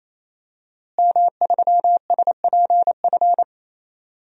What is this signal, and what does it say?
Morse code, heard as a keyed tone, about 27 wpm, 700 Hz. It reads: M3SPF
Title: DXCC-100 Call Signs - 5 Characters - Repeated 3x - 28wpm